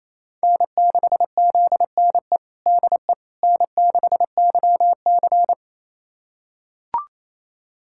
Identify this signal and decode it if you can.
Morse code, heard as a keyed tone, about 28 wpm, 700 Hz. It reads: N6ZNE DE N6YC
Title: USA Callsign DE Callsign 28wpm